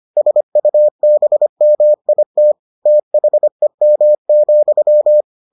Morse code, heard as a keyed tone, about 25 wpm, 600 Hz. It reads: SUBMIT THEM,